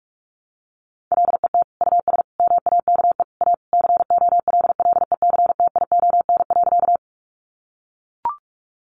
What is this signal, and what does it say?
Morse code, heard as a keyed tone, about 44 wpm, 700 Hz. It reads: LET US MAKE A COLLECTION.
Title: Sentences from Top 1200 Words 45wpm